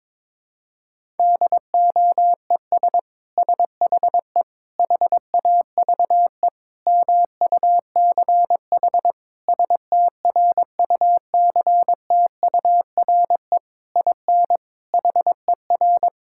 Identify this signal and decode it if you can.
Morse code, heard as a keyed tone, about 22 wpm, 700 Hz. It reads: DOES SHE HAVE MUCH STRUCTURE IN HER
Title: Sentences from Top 1000 Words - Review Part 6 - 22wpm